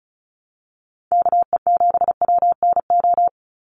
Morse code, heard as a keyed tone, about 34 wpm, 700 Hz. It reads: KE7WNO
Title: USA Call Signs 35wpm